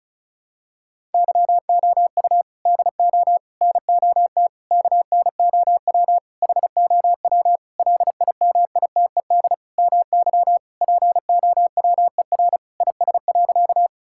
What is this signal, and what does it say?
Morse code, heard as a keyed tone, about 36 wpm, 700 Hz. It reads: YOU DO NOT KNOW HOW LIMITED MY POWER IS.
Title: Sentences from Top 1000 Words - Review Part 6 - 35wpm